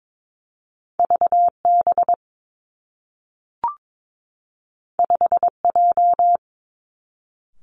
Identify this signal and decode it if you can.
Morse code, heard as a keyed tone, about 22 wpm, 700 Hz. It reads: V B 5 J
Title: Two Challenging Letter-Number Combinations 22wpm